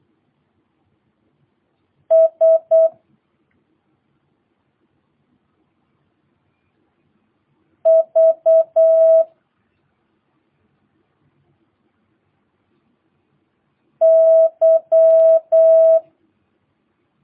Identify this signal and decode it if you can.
Morse code, heard as a keyed tone, about 8 wpm, 650 Hz. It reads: S V Y
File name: jh8cbh / 2022cw